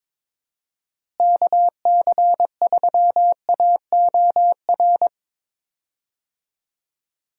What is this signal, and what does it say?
Morse code, heard as a keyed tone, about 22 wpm, 700 Hz. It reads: KC3AOR
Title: USA Call Signs - 6 Characters - Repeated 2x - 22wpm